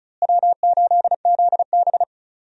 Morse code, heard as a keyed tone, about 34 wpm, 700 Hz. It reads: W8ZB
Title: USA Call Signs - 4 Characters - Repeated 3x - 35wpm